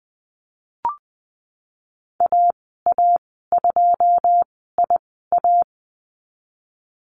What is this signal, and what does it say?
Morse code, heard as a keyed tone, about 20 wpm, spaced as for 15 wpm, 700 Hz. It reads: AA2IA
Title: 088 Lesson 15g - Number 5 - Call Signs 20wpm